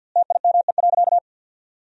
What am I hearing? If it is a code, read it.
Morse code, heard as a keyed tone, about 53 wpm, 700 Hz. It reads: TIME.
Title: 4x Word-Spacing - Sentences from Top 200 Words - 50wpm